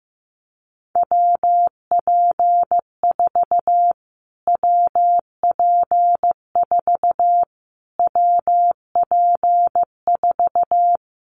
Morse code, heard as a keyed tone, about 15 wpm, 700 Hz. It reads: WP4 WP4 WP4
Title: Callsign Prefixes for Top 25 DXCC - Repeated 3x - 15wpm